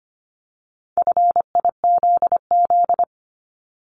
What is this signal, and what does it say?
Morse code, heard as a keyed tone, about 25 wpm, 700 Hz. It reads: FIZZ
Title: Multi-letter Phonemes 25wpm